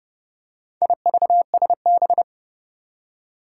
Morse code, heard as a keyed tone, about 30 wpm, 700 Hz. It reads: IVSB
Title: Four Challenging Letter-Number Combinations 30wpm